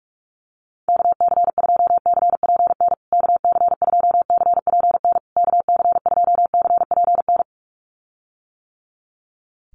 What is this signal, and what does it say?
Morse code, heard as a keyed tone, about 44 wpm, 700 Hz. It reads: KC2CPN KC2CPN KC2CPN
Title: USA Call Signs Repeated 3x 45wpm